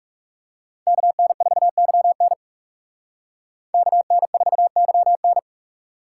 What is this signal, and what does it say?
Morse code, heard as a keyed tone, about 36 wpm, 700 Hz. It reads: KN4YN KN4YN
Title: Speed Racing - USA Call Signs 30wpm